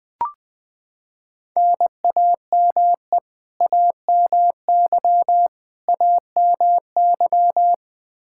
Morse code, heard as a keyed tone, about 20 wpm, 700 Hz. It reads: NAME AMY AMY
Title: Practice QSO - Exchange 1 - Beginning - 20wpm